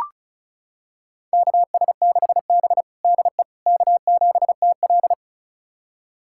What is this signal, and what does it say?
Morse code, heard as a keyed tone, about 36 wpm, 700 Hz. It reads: KS6B DE K7TL
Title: USA Callsign DE Callsign - 4 Characters - 35wpm